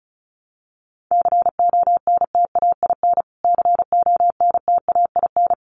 Morse code, heard as a keyed tone, about 34 wpm, 700 Hz. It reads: CONTAIN CONTAIN
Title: Top 500 Words Repeated 35wpm